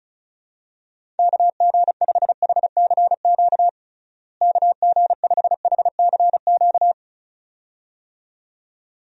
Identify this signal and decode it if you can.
Morse code, heard as a keyed tone, about 36 wpm, 700 Hz. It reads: KG5HCQ KG5HCQ
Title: USA Call Signs - 6 Characters - Repeated 2x - 35wpm